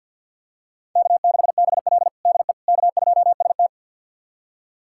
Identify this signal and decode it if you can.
Morse code, heard as a keyed tone, about 47 wpm, 700 Hz. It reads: K6BL DE K2ST
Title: USA Callsign DE Callsign - 4 Characters - 50wpm